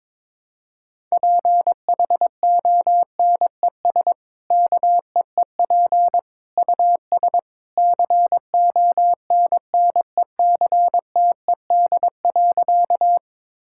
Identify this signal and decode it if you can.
Morse code, heard as a keyed tone, about 22 wpm, 700 Hz. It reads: PHONES KEEP US CONNECTED.
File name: practice / 22wpm / Easy Sentences with Hints - 22wpm